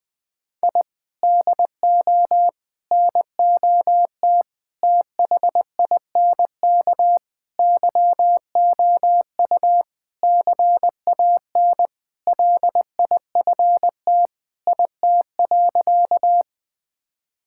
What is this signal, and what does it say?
Morse code, heard as a keyed tone, about 20 wpm, 700 Hz. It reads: I DO NOT THINK YOU CAN LIFT IT.
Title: Sentences with 5-Word Phrases - Spoken Beginning - 20wpm